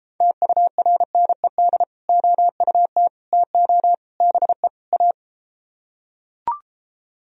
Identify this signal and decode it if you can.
Morse code, heard as a keyed tone, about 33 wpm, 700 Hz. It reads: TURNED OUT TO BE A